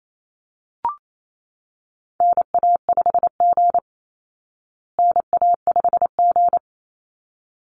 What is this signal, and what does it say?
Morse code, heard as a keyed tone, about 28 wpm, 700 Hz. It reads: NA5G NA5G
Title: Call Signs Repeated 3x - 28wpm